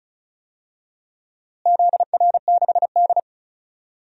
Morse code, heard as a keyed tone, about 36 wpm, 700 Hz. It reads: ZR6D
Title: DXCC-100 Call Signs - 4 Characters - Repeated 2x - 35wpm